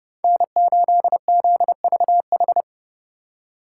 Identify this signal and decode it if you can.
Morse code, heard as a keyed tone, about 30 wpm, 700 Hz. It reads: N8ZVH